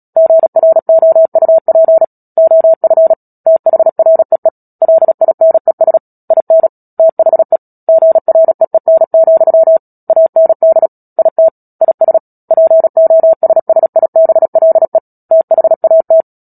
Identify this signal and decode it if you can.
Morse code, heard as a keyed tone, about 37 wpm, 650 Hz. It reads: GROUP OF THREE LINES IN THE GREEN, AND IT IS POSSIBLE THAT
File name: Book Two / 10-Chapter 10 - Epilogue - 36 WPM